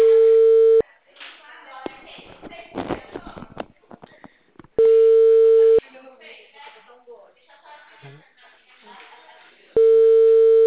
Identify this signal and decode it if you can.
Ringback tone (China, 450 Hz), with speech.